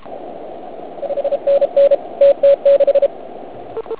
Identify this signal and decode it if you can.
Morse code, heard as a keyed tone, about 32 wpm, 560 Hz. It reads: 5NN TT6
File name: Archiv / CUC106 / cuc106 PILEUP